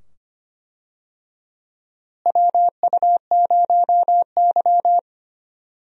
Morse code, heard as a keyed tone, about 25 wpm, 700 Hz. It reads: WU0Y
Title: USA Call Signs - 4 Characters - Repeated 2x - 25wpm